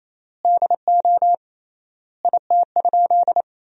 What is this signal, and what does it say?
Morse code, heard as a keyed tone, about 28 wpm, 700 Hz. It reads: DO IT?
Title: 3x Word-Spacing - Sentences from Top 100 Words - 28wpm